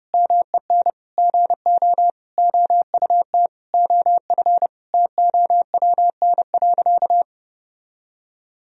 Morse code, heard as a keyed tone, about 30 wpm, 700 Hz. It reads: MEN GO OUT OF TOWN.
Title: Sentences from Top 1000 Words - Review Part 4 - 30wpm